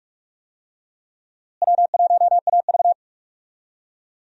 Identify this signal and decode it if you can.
Morse code, heard as a keyed tone, about 42 wpm, 700 Hz. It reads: W1AV
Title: USA Call Signs - 4 Characters - Repeated 2x - 45wpm